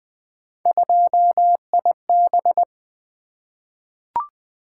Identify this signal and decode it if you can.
Morse code, heard as a keyed tone, about 20 wpm, 700 Hz. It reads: 2IB
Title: Three Letter-Number Combinations - 20wpm